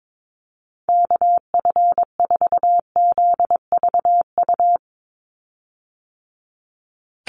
Morse code, heard as a keyed tone, about 22 wpm, 700 Hz. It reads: KF4ZVU
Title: USA Call Signs 22wpm